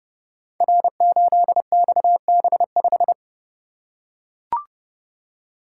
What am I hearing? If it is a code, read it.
Morse code, heard as a keyed tone, about 28 wpm, 700 Hz. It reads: R8XB5
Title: Code Groups - 5 Letter:Numbers - 30wpm